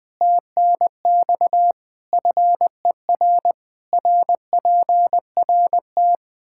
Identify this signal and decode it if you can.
Morse code, heard as a keyed tone, about 21 wpm, 700 Hz. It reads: TNX FER RPRT
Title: Practice QSO - Exchange 2 - Info - 20wpm